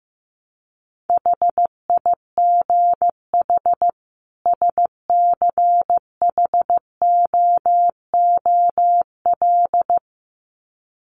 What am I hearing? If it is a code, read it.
Morse code, heard as a keyed tone, about 15 wpm, 700 Hz. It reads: HIGH SCHOOL
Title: Sets of 2 Words 15wpm